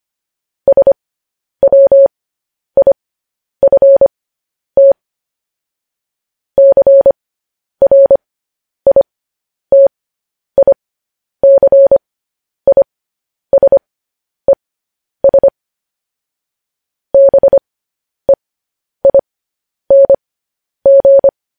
Morse code, heard as a keyed tone, about 25 wpm, spaced as for 10 wpm, 550 Hz. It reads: SWIFT CRITICISES BEING
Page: News Headlines in Morse Code at 10 WPM